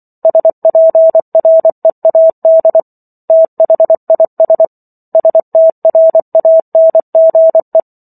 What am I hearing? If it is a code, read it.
Morse code, heard as a keyed tone, about 24 wpm, 650 Hz. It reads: SPREAD THIS STRANGE